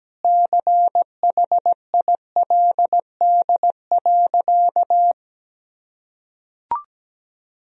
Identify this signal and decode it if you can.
Morse code, heard as a keyed tone, about 17 wpm, 700 Hz. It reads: CHILD.
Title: Sentences from Top 1000 Words - Review Part 3 - 17wpm